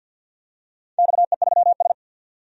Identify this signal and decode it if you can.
Morse code, heard as a keyed tone, about 52 wpm, 700 Hz. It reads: XE3S
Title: DXCC-100 Call Signs - 4 Characters - 50wpm